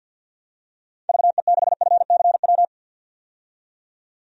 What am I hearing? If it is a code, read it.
Morse code, heard as a keyed tone, about 52 wpm, 700 Hz. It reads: VE6FCW
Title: DXCC-100 Call Signs - 6 Characters - 50wpm